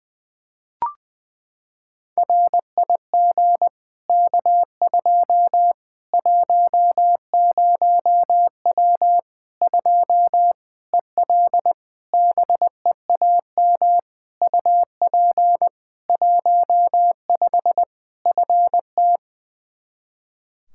Morse code, heard as a keyed tone, about 20 wpm, 700 Hz. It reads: RIG K2 10W 2 EL BEAM UP 15 FT
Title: Practice QSO - Exchange 2 - Info - 20wpm